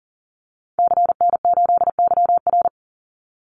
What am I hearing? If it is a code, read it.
Morse code, heard as a keyed tone, about 40 wpm, 700 Hz. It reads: CN8YR
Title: Call Signs Repeated 2x - 40wpm